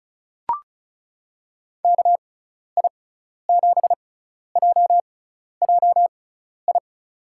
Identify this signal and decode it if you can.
Morse code, heard as a keyed tone, about 34 wpm, spaced as for 12 wpm, 700 Hz. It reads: KI7JJI
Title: USA Call Signs - 6x Character Spacing - 35wpm